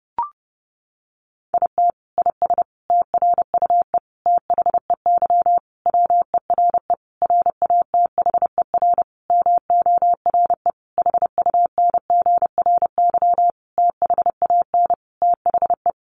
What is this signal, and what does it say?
Morse code, heard as a keyed tone, about 30 wpm, 700 Hz. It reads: IT IS TRUE THEY WERE RATHER MORE HUNGRY THAN THE
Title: Sentences from Top 1400 Words 30wpm